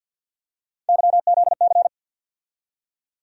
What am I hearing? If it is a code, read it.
Morse code, heard as a keyed tone, about 52 wpm, 700 Hz. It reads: YZC
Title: USA Call Sign Postfixes - 3 Letters 50wpm